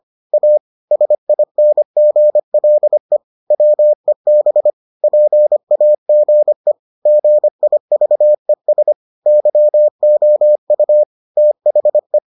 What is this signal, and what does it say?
Morse code, heard as a keyed tone, about 25 wpm, 600 Hz. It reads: A SINGLE WEB PAGE GIVES YOU THE